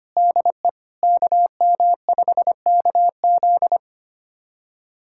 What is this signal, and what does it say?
Morse code, heard as a keyed tone, about 25 wpm, 700 Hz. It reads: DE KM5KZ
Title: USA Callsign DE Callsign - 5 Characters - 25wpm